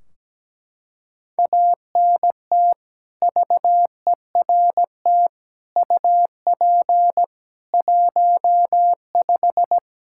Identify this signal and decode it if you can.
Morse code, heard as a keyed tone, about 17 wpm, 700 Hz. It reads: ANT VERT UP 15
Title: K1USN SST QSO Practice 17wpm